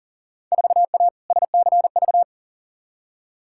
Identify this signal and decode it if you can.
Morse code, heard as a keyed tone, about 37 wpm, 700 Hz. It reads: 4A SCV
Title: Field Day Exchanges - 40wpm